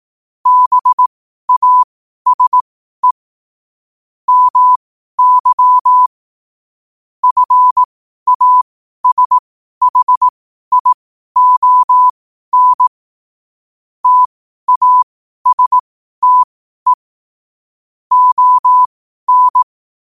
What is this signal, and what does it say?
Morse code, heard as a keyed tone, about 18 wpm, spaced as for 12 wpm, 1000 Hz. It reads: BASE MY FASHION TASTE ON
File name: qotdfiles / 12wpm